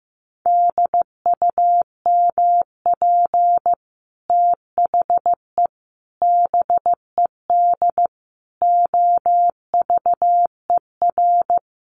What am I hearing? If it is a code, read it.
Morse code, heard as a keyed tone, about 15 wpm, 700 Hz. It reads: DUMP THE BED OVER
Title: Sentences from Top 1300 Words 15wpm